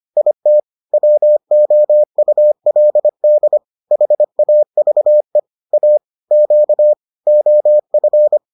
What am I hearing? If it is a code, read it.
Morse code, heard as a keyed tone, about 25 wpm, 600 Hz. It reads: IT WOULD HAVE A Q OF